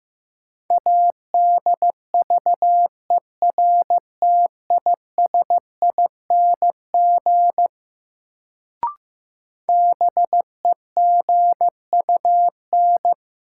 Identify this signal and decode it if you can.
Morse code, heard as a keyed tone, about 15 wpm, 700 Hz. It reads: ADVERTISING BEGUN
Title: Sentences from Top 1000 Words - Review Part 6 - New Word Drill - 15wpm